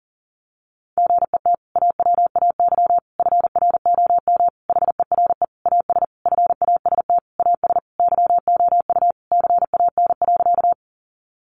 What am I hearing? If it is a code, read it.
Morse code, heard as a keyed tone, about 40 wpm, 700 Hz. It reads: GET AWAY FROM HERE AS FAST AS YOU CAN.
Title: Sentences from Top 600 Words 40wpm